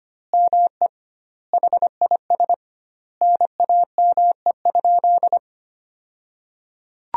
Morse code, heard as a keyed tone, about 25 wpm, 700 Hz. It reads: ME HIS NAME?
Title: 2x Word-Spacing - Sentences from Top 400 Words - 25wpm